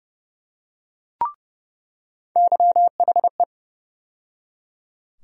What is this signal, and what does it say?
Morse code, heard as a keyed tone, about 30 wpm, 700 Hz. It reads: YHE
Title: USA Call Sign Postfixes - 3 Letters 30wpm